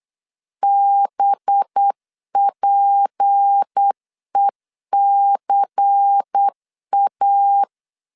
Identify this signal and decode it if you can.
Morse code, heard as a keyed tone, about 8 wpm, 800 Hz. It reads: BPECA
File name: LEZIONI / 6 PKX0 / 6 PKX 8wpm 3041